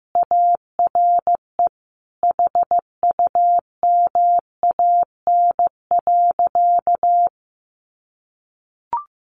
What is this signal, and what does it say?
Morse code, heard as a keyed tone, about 15 wpm, 700 Hz. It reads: ARE HUMAN.
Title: Sentences from Top 600 Words 15wpm